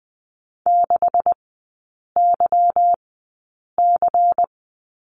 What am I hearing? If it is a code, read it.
Morse code, heard as a keyed tone, about 20 wpm, spaced as for 8 wpm, 690 Hz. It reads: 6YC
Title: 145 Lesson 24e - Letter y - Three Characters 20wpm